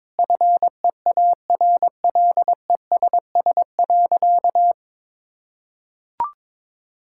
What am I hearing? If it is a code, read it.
Morse code, heard as a keyed tone, about 21 wpm, 700 Hz. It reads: FEARLESS.
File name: practice / 22wpm / Sentences with 4-Word Phrases - 22wpm